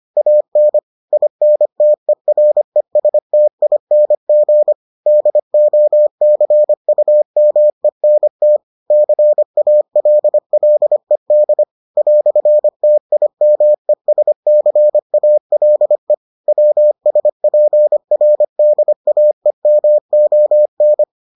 Morse code, heard as a keyed tone, about 25 wpm, 600 Hz. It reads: AN INTERESTING DOCUMENT CALLED "TIMESCALE WSPRDAEMON